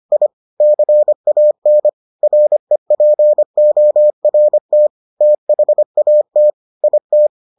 Morse code, heard as a keyed tone, about 25 wpm, 600 Hz. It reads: I CAN REPORT THAT IT